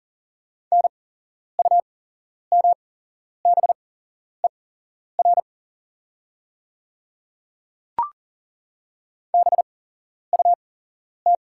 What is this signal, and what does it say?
Morse code, heard as a keyed tone, about 40 wpm, spaced as for 11 wpm, 700 Hz. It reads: NUMBER BUT
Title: 8x Character Spacing - Top 100 Words - 40wpm